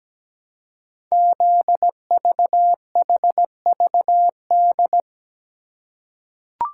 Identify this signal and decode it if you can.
Morse code, heard as a keyed tone, about 17 wpm, 700 Hz. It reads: ZVHVD
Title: Code Groups - 5 Letters - 17wpm